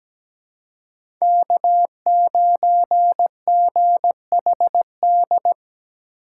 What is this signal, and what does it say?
Morse code, heard as a keyed tone, about 17 wpm, 700 Hz. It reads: K9GHD
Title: USA Call Signs - 3x Character Spacing - 17wpm